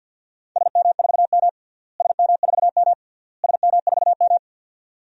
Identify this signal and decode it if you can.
Morse code, heard as a keyed tone, about 47 wpm, 700 Hz. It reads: SM4M SM4M SM4M
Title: DXCC-100 Call Signs - 4 Characters - Repeated 3x - 50wpm